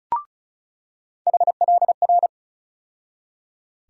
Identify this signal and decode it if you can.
Morse code, heard as a keyed tone, about 34 wpm, 700 Hz. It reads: HLR
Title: USA Call Sign Postfixes - 3 Letters 35wpm